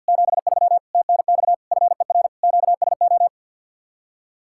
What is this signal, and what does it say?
Morse code, heard as a keyed tone, about 66 wpm, spaced as for 58 wpm, 700 Hz. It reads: 73 TNX FER QSO
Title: Speed-Racing Common QSO Phrases - 50wpm